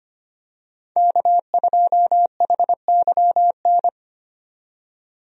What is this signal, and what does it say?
Morse code, heard as a keyed tone, about 25 wpm, 700 Hz. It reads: K2HYN
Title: USA Call Signs - 5 Characters - 25wpm